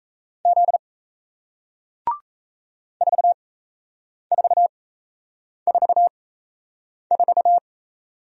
Morse code, heard as a keyed tone, about 37 wpm, spaced as for 8 wpm, 700 Hz. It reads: Z 4444